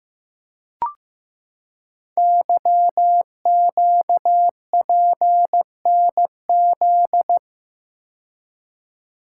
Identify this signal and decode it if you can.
Morse code, heard as a keyed tone, about 15 wpm, 700 Hz. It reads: YQPNZ
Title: Code Groups - 5 Letters - 15wpm